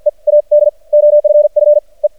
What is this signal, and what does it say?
Morse code, heard as a keyed tone, about 50 wpm, spaced as for 36 wpm, 590 Hz. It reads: EAD OFFE
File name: archived / CRSnet